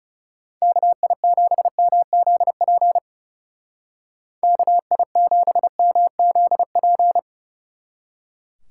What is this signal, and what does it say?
Morse code, heard as a keyed tone, about 36 wpm, 700 Hz. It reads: KI7MZP KI7MZP
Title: Speed Racing - USA 2x3 Call Signs 30wpm